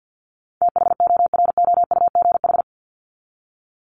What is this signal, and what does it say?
Morse code, heard as a keyed tone, about 50 wpm, 700 Hz. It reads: THOROUGH
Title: LY Suffix - Top 100 Words with Base Words 50wpm